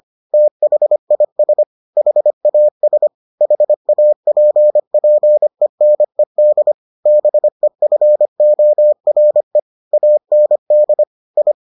Morse code, heard as a keyed tone, about 25 wpm, 600 Hz. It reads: THIS HAS HAPPENED BEFORE AND I